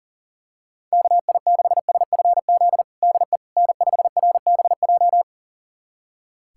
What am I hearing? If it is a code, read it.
Morse code, heard as a keyed tone, about 40 wpm, 700 Hz. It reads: KI6SFZ DE N5RBJ
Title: USA Callsign DE Callsign 40wpm